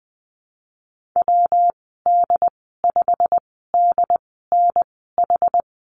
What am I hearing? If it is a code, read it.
Morse code, heard as a keyed tone, about 20 wpm, spaced as for 15 wpm, 700 Hz. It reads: WD5DNH
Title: 173 Lesson 28g - Number 9 - Call Signs 20wpm